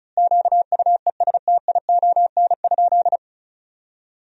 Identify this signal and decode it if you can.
Morse code, heard as a keyed tone, about 36 wpm, 700 Hz. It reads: QUESTION?
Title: 2x Word-Spacing - Sentences from Top 400 Words - 35wpm